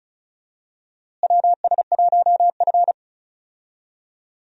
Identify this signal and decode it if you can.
Morse code, heard as a keyed tone, about 36 wpm, 700 Hz. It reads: WS1F
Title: USA Call Signs - 4 Characters - Repeated 2x - 35wpm